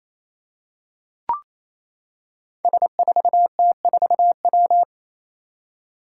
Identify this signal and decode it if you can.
Morse code, heard as a keyed tone, about 28 wpm, 700 Hz. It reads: S4T4W
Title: Five Challenging Letter-Number Combinations 28wpm